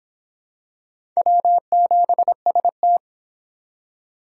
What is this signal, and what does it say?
Morse code, heard as a keyed tone, about 26 wpm, 700 Hz. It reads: W7ST